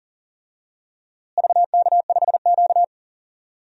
Morse code, heard as a keyed tone, about 37 wpm, 700 Hz. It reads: VK5Q
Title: DXCC-100 Call Signs - 4 Characters - Repeated 2x - 40wpm